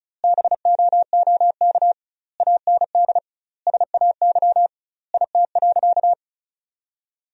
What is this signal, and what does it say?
Morse code, heard as a keyed tone, about 34 wpm, 700 Hz. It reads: BOOK AND SAY IT.
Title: 2x Word-Spacing - Sentences from Top 400 Words - 35wpm